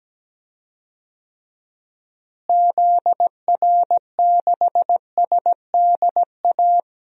Morse code, heard as a keyed tone, about 17 wpm, 700 Hz. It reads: ZR6SDA
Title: DXCC-100 Call Signs - 6 Characters - Repeated 3x - 17wpm